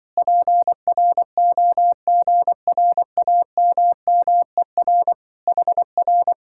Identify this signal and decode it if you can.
Morse code, heard as a keyed tone, about 24 wpm, 700 Hz. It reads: PROGRAMMER HR
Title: Speed-Racing Common QSO Phrases - 20wpm